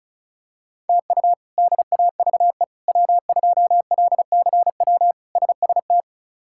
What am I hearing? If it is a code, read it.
Morse code, heard as a keyed tone, about 36 wpm, 700 Hz. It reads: TU DAVE W2LCW SST
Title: K1USN SST QSO Practice - 35wpm